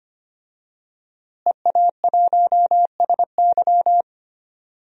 Morse code, heard as a keyed tone, about 25 wpm, 700 Hz. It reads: EA1SY
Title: DXCC-100 Call Signs - 5 Characters - Repeated 3x - 25wpm